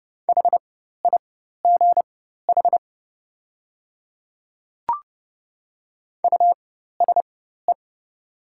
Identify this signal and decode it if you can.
Morse code, heard as a keyed tone, about 30 wpm, spaced as for 14 wpm, 700 Hz. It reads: HIGH USE
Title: 4x Character Spacing - Top 300 Words - 30wpm